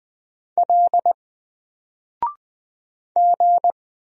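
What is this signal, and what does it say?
Morse code, heard as a keyed tone, about 20 wpm, 700 Hz. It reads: L G
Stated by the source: Single letter and number Morse drill at 20 wpm, from N4LCD.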